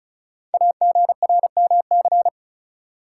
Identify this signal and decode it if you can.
Morse code, heard as a keyed tone, about 36 wpm, 690 Hz. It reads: AGRMC
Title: Code Groups - 5 Letters - 35wpm